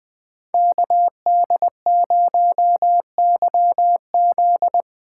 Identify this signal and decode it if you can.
Morse code, heard as a keyed tone, about 20 wpm, 700 Hz. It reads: KD0YZ
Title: Speed Racing - USA 2x2 Call Signs 17wpm